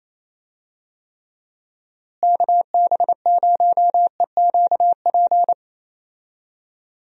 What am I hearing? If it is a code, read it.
Morse code, heard as a keyed tone, about 28 wpm, 700 Hz. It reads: KB0EQP
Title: USA Call Signs - 6 Characters - Repeated 2x - 28wpm